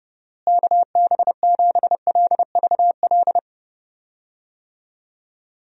Morse code, heard as a keyed tone, about 31 wpm, 700 Hz. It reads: KB7LVL
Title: USA Call Signs - 6 Characters - Repeated 3x - 30wpm